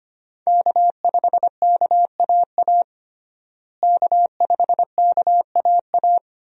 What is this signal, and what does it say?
Morse code, heard as a keyed tone, about 25 wpm, 700 Hz. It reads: K5KAA K5KAA
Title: USA Call Signs - 5 Characters - Repeated 2x - 25wpm